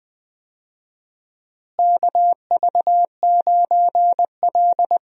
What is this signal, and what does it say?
Morse code, heard as a keyed tone, about 20 wpm, 700 Hz. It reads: KV9L
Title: USA Call Signs - 4 Characters - Repeated 2x - 20wpm